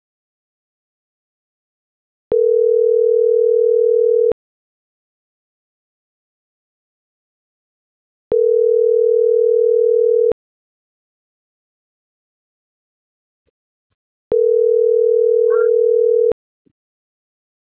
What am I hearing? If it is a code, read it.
Ringback tone (North America), 440 + 480 Hz.